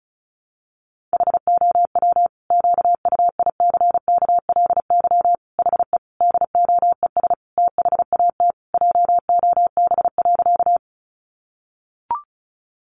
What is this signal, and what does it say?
Morse code, heard as a keyed tone, about 34 wpm, 700 Hz. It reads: HOW QUICKLY HE DOES THAT JOB.
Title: Sentences from Top 700 Words 35wpm